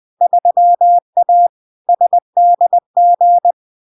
Morse code, heard as a keyed tone, about 20 wpm, 700 Hz. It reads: 3A SDG
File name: LARK-FD-2022-Log-in-CW-20wpm-700-Hz